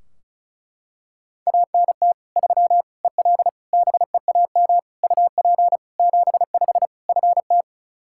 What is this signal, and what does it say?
Morse code, heard as a keyed tone, about 34 wpm, 700 Hz. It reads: ANT 3 EL BEAM UP 75 FT
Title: Practice QSO - Exchange 2 - Info - 35wpm